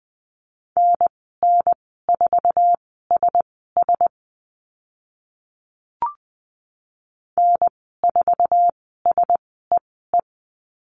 Morse code, heard as a keyed tone, about 20 wpm, spaced as for 14 wpm, 700 Hz. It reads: NN4SS N4SEE
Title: 048 Lesson 9g - Number 4 - Call Signs 20wpm